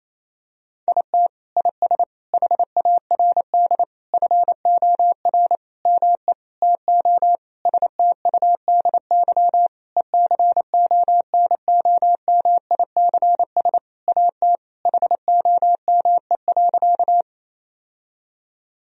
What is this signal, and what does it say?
Morse code, heard as a keyed tone, about 28 wpm, 700 Hz. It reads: IT IS HARD FOR ME TO STUDY ECONOMICS AT HOME.
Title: Sentences from Top 1000 Words - Review Part 3 - 28wpm